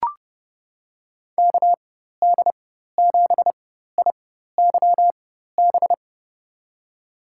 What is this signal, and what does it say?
Morse code, heard as a keyed tone, about 31 wpm, spaced as for 14 wpm, 700 Hz. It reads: KD7IYB
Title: USA Call Signs - 6 Characters - 4x Character Spacing - 30wpm